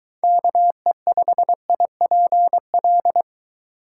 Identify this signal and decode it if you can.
Morse code, heard as a keyed tone, about 23 wpm, 700 Hz. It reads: KE5IPL